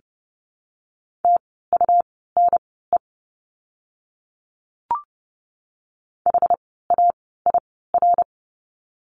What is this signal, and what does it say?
Morse code, heard as a keyed tone, about 30 wpm, spaced as for 17 wpm, 700 Hz. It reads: TUNE HAIR
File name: Lesson 17f - Letter u - Words 30wpm